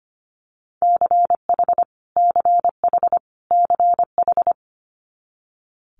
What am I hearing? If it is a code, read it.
Morse code, heard as a keyed tone, about 25 wpm, 700 Hz. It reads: CH CH CH
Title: Multi-letter Phonemes Repeated 3x 25wpm